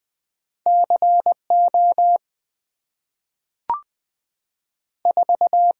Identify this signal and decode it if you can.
Morse code, heard as a keyed tone, about 20 wpm, 700 Hz. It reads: C O 4
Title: Two Letter-Number Combinations - 20wpm